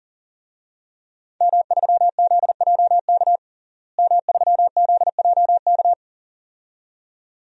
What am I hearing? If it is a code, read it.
Morse code, heard as a keyed tone, about 40 wpm, 700 Hz. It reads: M3ZJK M3ZJK
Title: DXCC-100 Call Signs - 5 Characters - Repeated 2x - 40wpm